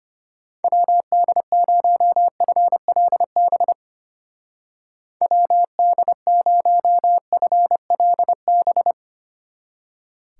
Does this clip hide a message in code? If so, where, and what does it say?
Yes: Morse code from 0.6 to 8.9 s, heard as a keyed tone, about 33 wpm, 700 Hz. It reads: WD0FLB WD0FLB